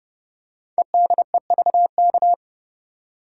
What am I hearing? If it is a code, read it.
Morse code, heard as a keyed tone, about 30 wpm, 690 Hz. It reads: EDEVK
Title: Code Groups - 5 Letters - 30wpm